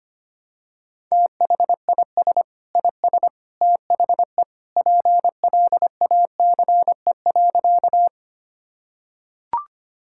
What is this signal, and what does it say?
Morse code, heard as a keyed tone, about 25 wpm, 700 Hz. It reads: THIS IS THE PLACE.
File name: practice / 25wpm / Sentences with 3-Word Phrases - 25wpm